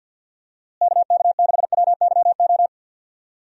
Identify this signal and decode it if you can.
Morse code, heard as a keyed tone, about 52 wpm, 700 Hz. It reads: KK6WYO
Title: USA Call Signs - 6 Characters - Repeated 2x - 50wpm